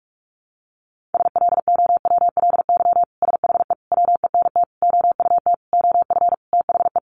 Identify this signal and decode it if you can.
Morse code, heard as a keyed tone, about 44 wpm, 700 Hz. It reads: SLOWLY SHE WENT OUT OF THE
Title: Sentences from Top 1100 Words 45wpm